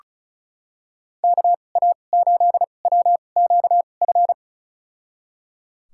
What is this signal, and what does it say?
Morse code, heard as a keyed tone, about 34 wpm, spaced as for 25 wpm, 700 Hz. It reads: KA8WQF
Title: USA Call Signs - 6 Characters - 2x Character Spacing - 35wpm